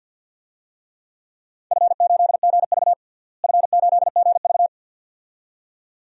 Morse code, heard as a keyed tone, about 53 wpm, 700 Hz. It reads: F8GV F8GV
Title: DXCC-100 Call Signs - 4 Characters - Repeated 2x - 50wpm